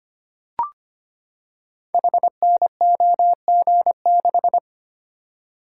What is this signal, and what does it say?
Morse code, heard as a keyed tone, about 25 wpm, 700 Hz. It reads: HNOG6
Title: Code Groups - 5 Letter:Numbers - 25wpm